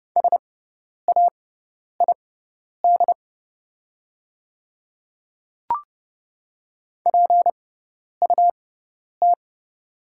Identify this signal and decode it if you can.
Morse code, heard as a keyed tone, about 30 wpm, spaced as for 10 wpm, 700 Hz. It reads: SAID PUT
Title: 6x Character Spacing - Top 200 Words - 30wpm